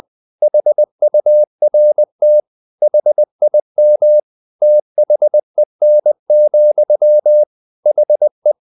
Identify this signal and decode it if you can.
Morse code, heard as a keyed tone, about 20 wpm, 600 Hz. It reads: HURT HIM THEN, HE
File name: the old man and the sea / the old man and the sea part 1 20wpm